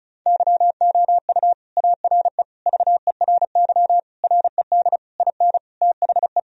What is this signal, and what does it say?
Morse code, heard as a keyed tone, about 36 wpm, 700 Hz. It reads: YOU ARE VERY RED IN THE
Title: Sentences from Top 1000 Words - Review Part 2 - 35wpm